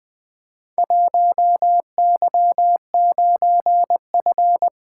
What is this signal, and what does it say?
Morse code, heard as a keyed tone, about 20 wpm, 700 Hz. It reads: 1Y9F
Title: Four Letter-Number Combinations - 20wpm